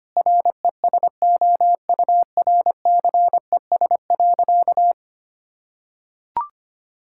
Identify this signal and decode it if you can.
Morse code, heard as a keyed tone, about 25 wpm, 700 Hz. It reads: RESOURCES.
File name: practice / 25wpm / Sentences with 2-Word Phrases - 25wpm